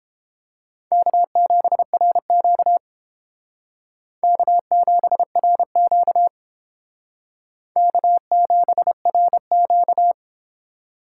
Morse code, heard as a keyed tone, about 30 wpm, 700 Hz. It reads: K7RQ K7RQ K7RQ